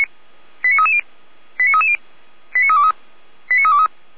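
Five-tone selective call (ZVEI), heard as quick runs of five tones, tones of about 70 ms. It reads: 89330 89330 89323 89323
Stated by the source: ZVEI five-tone selective call received on an SDR.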